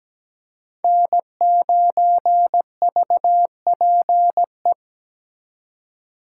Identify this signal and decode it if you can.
Morse code, heard as a keyed tone, about 20 wpm, 700 Hz. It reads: N9VPE
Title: Speed Racing - USA 1x3 Call Signs 17wpm